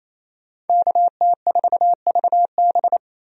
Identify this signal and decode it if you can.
Morse code, heard as a keyed tone, about 27 wpm, 700 Hz. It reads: KT4VB
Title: USA Call Signs - 5 Characters - 28wpm